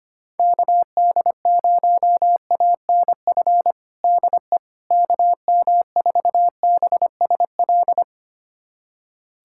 Morse code, heard as a keyed tone, about 25 wpm, 700 Hz. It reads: KD0ANF DE KM4BSL
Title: USA Callsign DE Callsign - 6 Characters - 25wpm